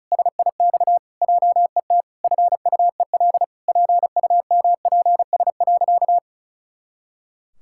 Morse code, heard as a keyed tone, about 36 wpm, 700 Hz. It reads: SIX JET FUEL PUMPS.
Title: Pangram Challenge v2 - 35wpm